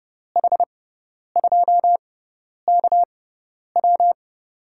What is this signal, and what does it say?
Morse code, heard as a keyed tone, about 32 wpm, spaced as for 10 wpm, 700 Hz. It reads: H2KW
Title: USA Call Signs - 5 Characters - 6x Character Spacing - 30wpm